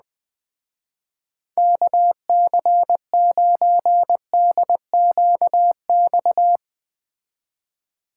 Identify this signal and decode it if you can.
Morse code, heard as a keyed tone, about 19 wpm, 700 Hz. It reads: KC9DQX
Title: Speed Racing - USA 2x3 Call Signs 17wpm